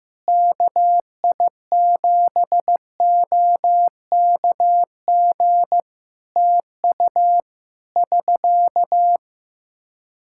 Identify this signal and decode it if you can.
Morse code, heard as a keyed tone, about 15 wpm, 700 Hz. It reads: KI7OKG TU <SK>
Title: Practice QSO - Exchange 4 - Ending - 15wpm